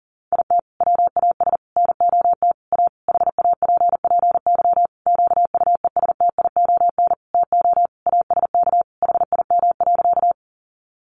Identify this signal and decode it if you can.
Morse code, heard as a keyed tone, about 40 wpm, 700 Hz. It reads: IT WAS NOT A HAPPY QUESTION TO ASK HIM.
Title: Sentences from Top 1000 Words 40wpm